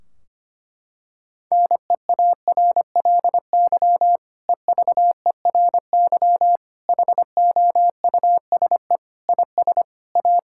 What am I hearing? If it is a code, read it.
Morse code, heard as a keyed tone, about 25 wpm, 700 Hz. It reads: NEARLY EVERY HOUSE IS A
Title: Sentences from Top 1600 Words 25wpm